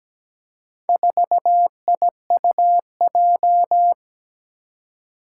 Morse code, heard as a keyed tone, about 17 wpm, 700 Hz. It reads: 4IUJ
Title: Four Challenging Letter-Number Combinations 17wpm